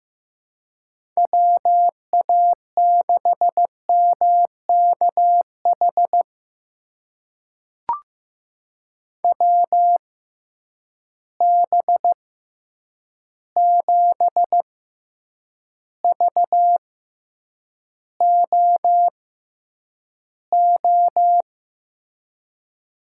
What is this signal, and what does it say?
Morse code, heard as a keyed tone, about 15 wpm, spaced as for 8 wpm, 700 Hz. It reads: WA6MKH WB7VOO
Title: USA Call Signs - 6 Characters - 6x Character Spacing - 15wpm